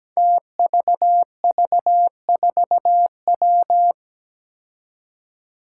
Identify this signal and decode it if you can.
Morse code, heard as a keyed tone, about 17 wpm, 700 Hz. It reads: TVV4W
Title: Five Challenging Letter-Number Combinations 17wpm